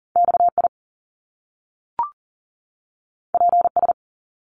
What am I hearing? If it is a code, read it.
Morse code, heard as a keyed tone, about 42 wpm, 700 Hz. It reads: XI PS
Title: Top Two Letter Combinations in Words 40wpm